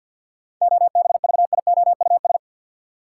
Morse code, heard as a keyed tone, about 53 wpm, 700 Hz. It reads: OBVIOUS
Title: Seven Letter Words - 50wpm